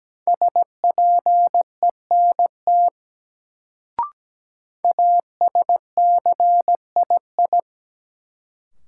Morse code, heard as a keyed tone, about 17 wpm, 700 Hz. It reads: SPENT ASCII
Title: Sets of 5 Words - New Word Drill - 17wpm